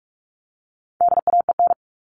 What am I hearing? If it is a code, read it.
Morse code, heard as a keyed tone, about 44 wpm, 700 Hz. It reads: DAEN
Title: All Cut Numbers - 4-Digit 45wpm